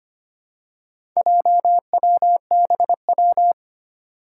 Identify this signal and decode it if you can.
Morse code, heard as a keyed tone, about 25 wpm, 700 Hz. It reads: JWBW
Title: Four Challenging Letter-Number Combinations 25wpm